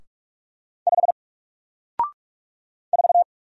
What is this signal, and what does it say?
Morse code, heard as a keyed tone, about 44 wpm, 700 Hz. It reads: 5 4
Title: Single Letter-Number 45wpm